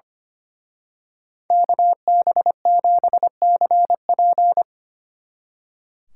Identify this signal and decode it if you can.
Morse code, heard as a keyed tone, about 28 wpm, 700 Hz. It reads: KB7CP
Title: Speed Racing - USA 2x2 Call Signs 25wpm